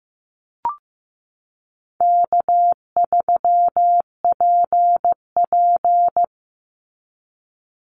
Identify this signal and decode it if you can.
Morse code, heard as a keyed tone, about 15 wpm, 700 Hz. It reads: K3PP
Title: Easy Call Signs Repeated 2x - 15wpm